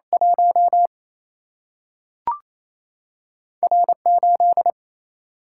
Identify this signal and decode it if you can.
Morse code, heard as a keyed tone, about 28 wpm, 700 Hz. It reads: 1 R 8
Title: Single-Letter Single-Number 28wpm